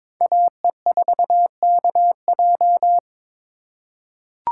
Morse code, heard as a keyed tone, about 22 wpm, 700 Hz. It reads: AE4KJ
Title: USA Call Signs - 5 Characters - 2x Character Spacing - 22wpm